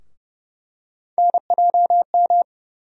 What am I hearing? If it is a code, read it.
Morse code, heard as a keyed tone, about 30 wpm, 700 Hz. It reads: NJM
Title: USA Call Sign Postfixes - 3 Letters 30wpm